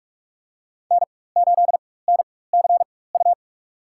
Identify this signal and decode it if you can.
Morse code, heard as a keyed tone, about 46 wpm, spaced as for 21 wpm, 700 Hz. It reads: N8NCU
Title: USA Call Signs - 5 Characters - 4x Character Spacing - 45wpm